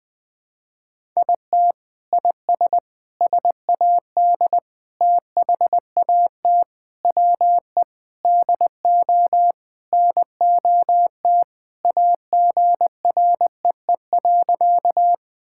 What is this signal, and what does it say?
Morse code, heard as a keyed tone, about 20 wpm, 700 Hz. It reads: IT IS SAD THAT WE DO NOT AGREE.
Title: Sentences with 4-Word Phrases - 20wpm